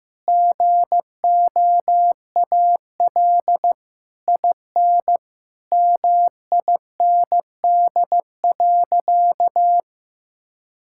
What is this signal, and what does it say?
Morse code, heard as a keyed tone, about 16 wpm, 700 Hz. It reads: GOAL IN MIND.
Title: Sentences from Top 1000 Words - Review Part 5 - 15wpm